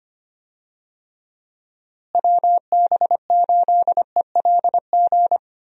Morse code, heard as a keyed tone, about 25 wpm, 700 Hz. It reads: WB8ELG
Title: USA Call Signs - 6 Characters - Repeated 2x - 25wpm